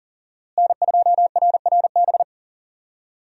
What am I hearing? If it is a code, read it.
Morse code, heard as a keyed tone, about 38 wpm, 700 Hz. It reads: N2RRB
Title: USA Call Signs - 5 Characters - Repeated 3x - 40wpm